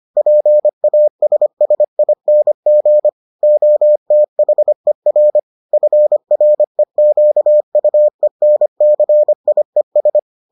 Morse code, heard as a keyed tone, about 25 wpm, 600 Hz. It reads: PASSING OTHER FREQUENCIES